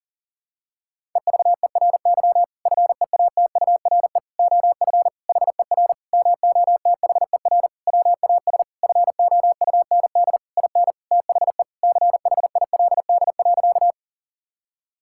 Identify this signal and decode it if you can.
Morse code, heard as a keyed tone, about 40 wpm, 700 Hz. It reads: EVERY FEATURE OF HER MOTHER WAS FOUND IN THE CHILD.
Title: Sentences from Top 1000 Words - Review Part 5 - 40wpm